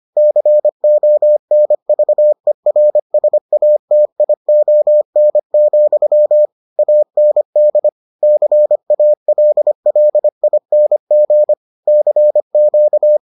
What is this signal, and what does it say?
Morse code, heard as a keyed tone, about 25 wpm, 600 Hz. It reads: CONVERSATION, AND CALLING CQ